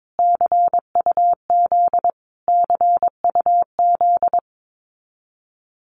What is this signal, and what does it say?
Morse code, heard as a keyed tone, about 22 wpm, 700 Hz. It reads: CUZ CUZ
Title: Abbreviations Repeated 3x - 22wpm